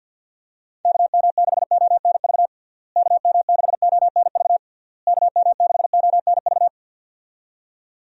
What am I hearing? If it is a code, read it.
Morse code, heard as a keyed tone, about 50 wpm, 700 Hz. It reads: KM6ONV KM6ONV KM6ONV
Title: USA Call Signs - 6 Characters - Repeated 3x - 50wpm